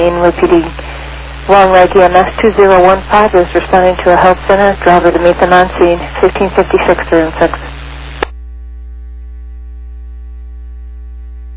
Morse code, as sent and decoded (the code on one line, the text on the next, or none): none